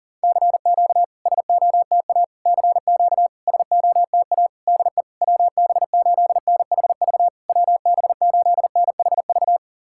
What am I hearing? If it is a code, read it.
Morse code, heard as a keyed tone, about 39 wpm, 700 Hz. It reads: CQ SOTA CQ SOTA DE WB8NHV WB8NHV
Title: SOTA QSO Practice 40wpm